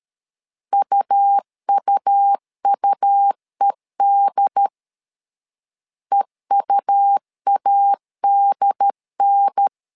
Morse code, heard as a keyed tone, about 13 wpm, 800 Hz. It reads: UUUED EUADN
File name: LEZIONI / 3 MWRO / 3 M 12wpm 1541